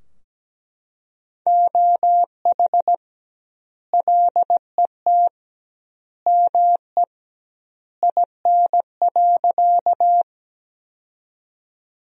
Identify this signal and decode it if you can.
Morse code, heard as a keyed tone, about 17 wpm, 700 Hz. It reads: OH LET ME IN.
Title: 2x Word-Spacing - Sentences from Top 500 Words - 17wpm